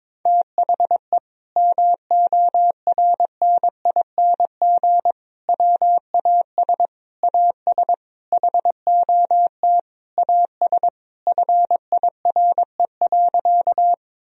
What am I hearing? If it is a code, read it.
Morse code, heard as a keyed tone, about 22 wpm, 700 Hz. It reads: THE MORNING WAS AS HOT AS FIRE.
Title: Sentences from Top 1000 Words - Review Part 3 - 22wpm